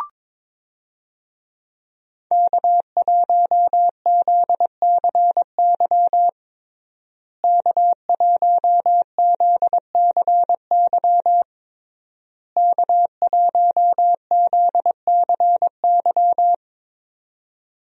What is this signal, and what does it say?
Morse code, heard as a keyed tone, about 22 wpm, 700 Hz. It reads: K1ZCY K1ZCY K1ZCY
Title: USA Call Signs - 5 Characters - Repeated 3x - 22wpm